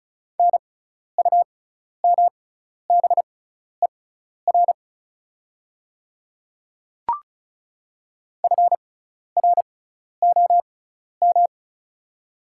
Morse code, heard as a keyed tone, about 34 wpm, spaced as for 12 wpm, 700 Hz. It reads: NUMBER FROM
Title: 6x Character Spacing - Top 200 Words - 35wpm